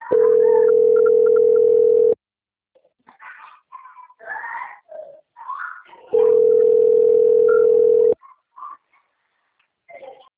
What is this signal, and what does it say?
Ringback tone (North America), 440 + 480 Hz.
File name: out- - - - - .54623